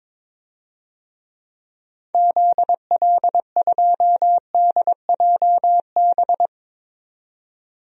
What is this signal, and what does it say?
Morse code, heard as a keyed tone, about 22 wpm, 700 Hz. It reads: ZL2DJB
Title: DXCC-100 Call Signs - 6 Characters - 22wpm